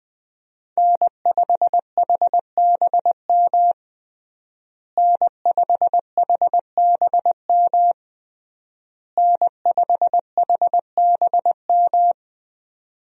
Morse code, heard as a keyed tone, about 20 wpm, 700 Hz. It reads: N5HBM N5HBM N5HBM
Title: USA Call Signs - 5 Characters - Repeated 3x - 20wpm